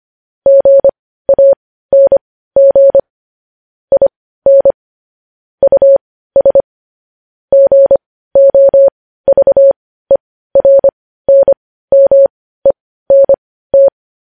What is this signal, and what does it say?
Morse code, heard as a keyed tone, about 25 wpm, spaced as for 15 wpm, 550 Hz. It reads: GANG IN US GOVERNMENT
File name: podcast / News headlines-1x-25x15